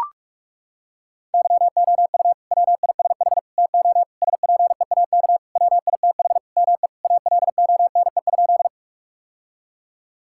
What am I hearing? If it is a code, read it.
Morse code, heard as a keyed tone, about 43 wpm, 700 Hz. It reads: YOU WISH TO SPEAK WITH ME ALONE?
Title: Sentences from Top 1600 Words 45wpm